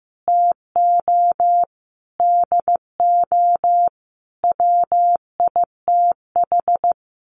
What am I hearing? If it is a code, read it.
Morse code, heard as a keyed tone, about 15 wpm, 700 Hz. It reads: TO DO WITH
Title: Sets of 5 Words 15wpm